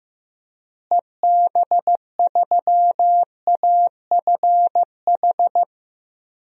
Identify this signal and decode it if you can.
Morse code, heard as a keyed tone, about 15 wpm, 700 Hz. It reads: EB3AFH
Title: DXCC-100 Call Signs - 6 Characters - Repeated 2x - 15wpm